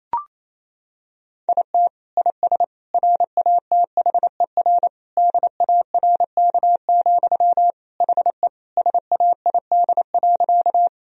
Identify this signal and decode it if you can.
Morse code, heard as a keyed tone, about 28 wpm, 700 Hz. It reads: IT IS RATHER DARK, HE SAID.
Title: Sentences from Top 1000 Words - Review Part 4 - 28wpm